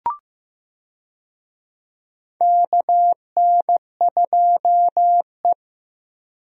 Morse code, heard as a keyed tone, about 15 wpm, 700 Hz. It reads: KN2E
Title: USA Call Signs - 4 Characters - Repeated 3x - 15wpm